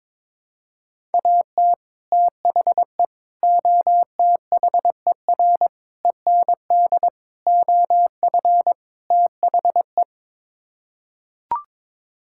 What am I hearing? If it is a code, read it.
Morse code, heard as a keyed tone, about 21 wpm, 700 Hz. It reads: AT THE OTHER END OF THE
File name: practice / 22wpm / Sets of 6 Words - 22wpm